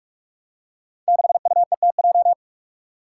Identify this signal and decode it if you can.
Morse code, heard as a keyed tone, about 42 wpm, 700 Hz. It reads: 6UETJ
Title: Five Challenging Letter-Number Combinations 45wpm